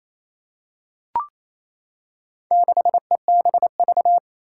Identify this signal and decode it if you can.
Morse code, heard as a keyed tone, about 27 wpm, 700 Hz. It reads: 6EBV
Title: Four Challenging Letter-Number Combinations 28wpm